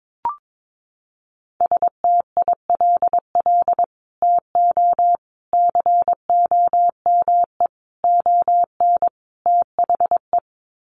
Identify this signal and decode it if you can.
Morse code, heard as a keyed tone, about 22 wpm, 700 Hz. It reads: STILL TO COME ON THE
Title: Sets of 5 Words Encore2 22wpm